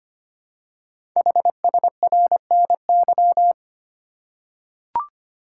Morse code, heard as a keyed tone, about 25 wpm, 700 Hz. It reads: HSRNY
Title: Code Groups - 5 Letters - 25wpm